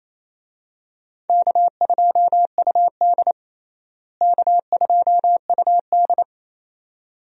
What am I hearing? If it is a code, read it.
Morse code, heard as a keyed tone, about 27 wpm, 700 Hz. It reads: K2UD K2UD
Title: USA Call Signs - 4 Characters - Repeated 2x - 28wpm